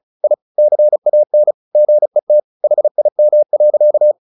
Morse code, heard as a keyed tone, about 34 wpm, 600 Hz. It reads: I CAN GET HIM.
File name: the old man and the sea 2 35wpm